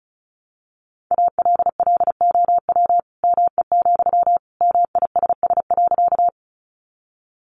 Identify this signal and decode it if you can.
Morse code, heard as a keyed tone, about 34 wpm, 700 Hz. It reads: ALLOW ME, MISS.
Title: Sentences from Top 700 Words 35wpm